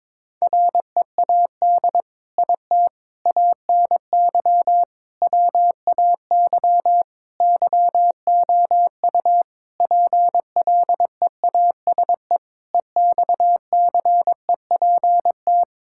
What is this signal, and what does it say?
Morse code, heard as a keyed tone, about 22 wpm, 700 Hz. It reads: READ IT ANY WAY YOU PLEASE EXCEPT
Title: Sentences from Top 1600 Words 22wpm